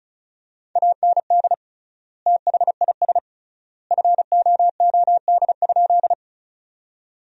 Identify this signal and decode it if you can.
Morse code, heard as a keyed tone, about 36 wpm, 700 Hz. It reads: AND THIS FOOD?
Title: 3x Word-Spacing - Sentences from Top 300 Words - 35wpm